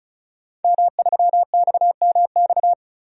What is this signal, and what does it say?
Morse code, heard as a keyed tone, about 36 wpm, 700 Hz. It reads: M3XMX
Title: DXCC-100 Call Signs - 5 Characters - 35wpm